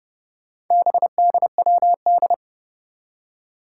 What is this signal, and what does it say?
Morse code, heard as a keyed tone, about 30 wpm, 700 Hz. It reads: BDWD
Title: Four Challenging Letter-Number Combinations 30wpm